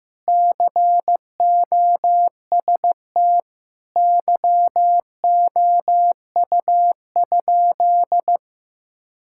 Morse code, heard as a keyed tone, about 15 wpm, 700 Hz. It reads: COST YOU?
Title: Sentences from Top 1000 Words - Review Part 5 - 15wpm